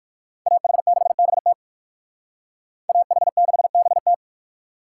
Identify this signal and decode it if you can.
Morse code, heard as a keyed tone, about 57 wpm, 700 Hz. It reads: AH6BT AH6BT